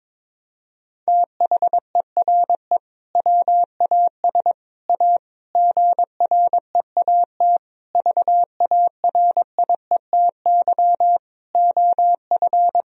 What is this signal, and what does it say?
Morse code, heard as a keyed tone, about 22 wpm, 700 Hz. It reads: THERE WAS A GREAT VARIETY OF
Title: Sentences from Top 1000 Words - Review Part 4 - 22wpm